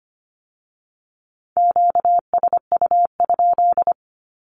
Morse code, heard as a keyed tone, about 25 wpm, 700 Hz. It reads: QSU?
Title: Q-Codes with Spoken Meaning - 25wpm